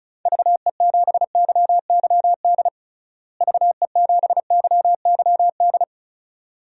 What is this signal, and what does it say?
Morse code, heard as a keyed tone, about 36 wpm, 700 Hz. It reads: VE7YYD VE7YYD
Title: DXCC-100 Call Signs - 6 Characters - Repeated 2x - 35wpm